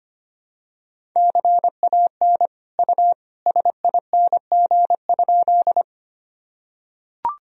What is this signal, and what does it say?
Morse code, heard as a keyed tone, about 25 wpm, 700 Hz. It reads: CAN U SING?
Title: Sentences from Top 400 Words and Abbreviations 25wpm